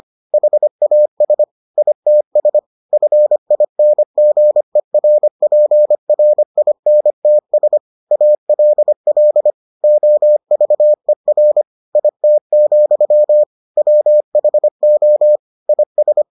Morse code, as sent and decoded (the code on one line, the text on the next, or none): .... .- ... / .. - ... / ..-. .. -. --. .-. .--. .-. .. -. - ... / .- .-.. .-.. / --- ...- . .-. / .. - --..-- / .-- .... --- / .. ...
HAS ITS FINGERPRINTS ALL OVER IT, WHO IS